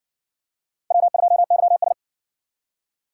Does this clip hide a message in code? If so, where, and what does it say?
Yes: Morse code from 0.9 to 1.9 s, heard as a keyed tone, about 57 wpm, 700 Hz. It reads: W2YS